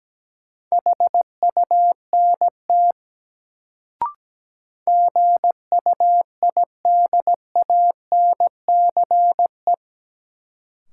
Morse code, heard as a keyed tone, about 17 wpm, 700 Hz. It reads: HUNT GUIDANCE
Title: Sentences from Top 1600 Words - New Word Drill Alt - 17wpm